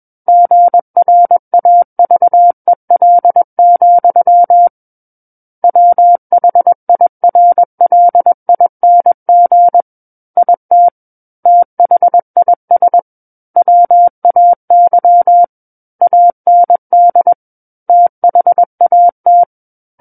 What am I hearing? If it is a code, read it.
Morse code, heard as a keyed tone, about 21 wpm, 700 Hz. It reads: GRAVEL, WHIRLING IT THIS WAY AND THAT
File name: Book One / 12-Chapter 12 - 21 WPM